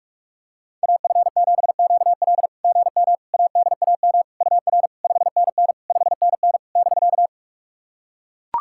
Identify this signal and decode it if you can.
Morse code, heard as a keyed tone, about 44 wpm, 690 Hz. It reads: AF7QL GM ADAM UR 5NN 5NN <BK>
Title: SOTA QSO Practice 45wpm